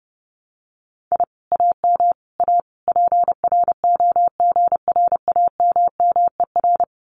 Morse code, heard as a keyed tone, about 30 wpm, 700 Hz. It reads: I AM A PROGRAMMER
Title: Common QSO Phrases Encore 30wpm